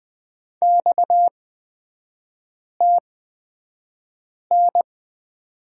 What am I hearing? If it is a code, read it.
Morse code, heard as a keyed tone, about 20 wpm, spaced as for 5 wpm, 690 Hz. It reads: XTN
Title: 229 Lesson 37e - Letter x - Three Characters 20wpm